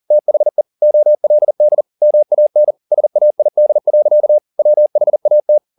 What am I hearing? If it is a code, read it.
Morse code, heard as a keyed tone, about 40 wpm, 600 Hz. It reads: THE OLD MAN SAID. WHAT